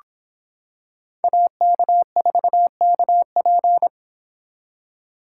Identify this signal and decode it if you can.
Morse code, heard as a keyed tone, about 26 wpm, 700 Hz. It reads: AK4KP